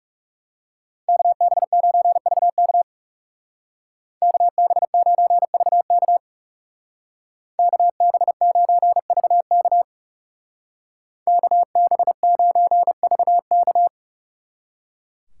Morse code, heard as a keyed tone, about 37 wpm, 700 Hz. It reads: KB9VK KB9VK KB9VK KB9VK
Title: Speed Racing - USA 2x2 Call Signs 30wpm